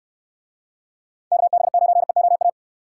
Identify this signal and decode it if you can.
Morse code, heard as a keyed tone, about 69 wpm, 700 Hz. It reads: KB9EOA